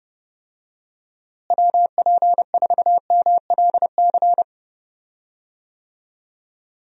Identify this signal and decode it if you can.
Morse code, heard as a keyed tone, about 30 wpm, 700 Hz. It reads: WP4MLC
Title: USA Call Signs - 6 Characters - Repeated 2x - 30wpm